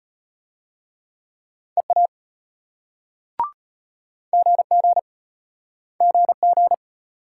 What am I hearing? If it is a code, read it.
Morse code, heard as a keyed tone, about 32 wpm, 700 Hz. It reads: EA GG GG